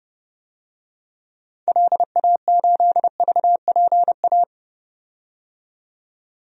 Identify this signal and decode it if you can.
Morse code, heard as a keyed tone, about 31 wpm, 700 Hz. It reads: LA8VPA
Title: DXCC-100 Call Signs - 6 Characters - 30wpm